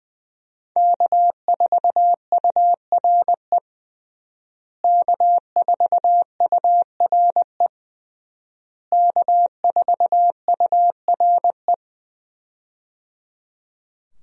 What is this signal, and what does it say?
Morse code, heard as a keyed tone, about 20 wpm, 700 Hz. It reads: K4URE K4URE K4URE
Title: USA Call Signs - 5 Characters - Repeated 3x - 20wpm